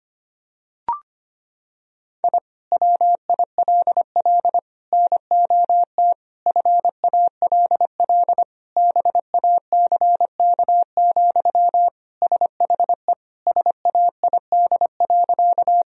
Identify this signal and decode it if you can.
Morse code, heard as a keyed tone, about 25 wpm, 700 Hz. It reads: I WILL NOT FALL BACK, SHE SAID.
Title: Sentences from Top 1000 Words - Review Part 2 - 25wpm